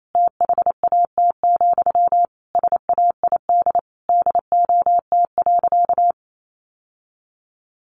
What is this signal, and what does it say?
Morse code, heard as a keyed tone, about 28 wpm, 700 Hz. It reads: THAT, SAID DOT.
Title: Sentences from Top 1500 Words 28wpm